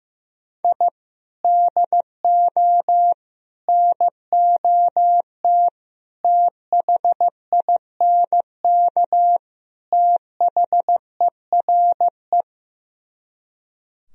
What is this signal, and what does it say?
Morse code, heard as a keyed tone, about 15 wpm, 700 Hz. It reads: I DO NOT THINK THERE
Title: Speed-Racing - Sets of 5 Words 15wpm